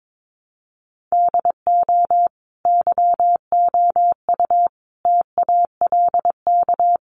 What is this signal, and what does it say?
Morse code, heard as a keyed tone, about 22 wpm, 700 Hz. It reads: DO YOU TALK
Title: Sentences from Top 500 Words 22wpm